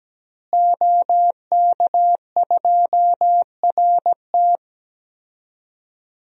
Morse code, heard as a keyed tone, about 17 wpm, 700 Hz. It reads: OK2RT
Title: DXCC-100 Call Signs - 5 Characters - Repeated 3x - 17wpm